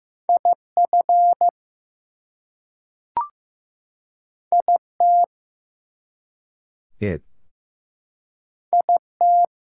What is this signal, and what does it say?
Morse code, heard as a keyed tone, about 15 wpm, 700 Hz. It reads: IF IT IT
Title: Two Letter Words - 15wpm